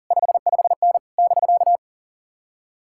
Morse code, heard as a keyed tone, about 51 wpm, 700 Hz. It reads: 55N <BK>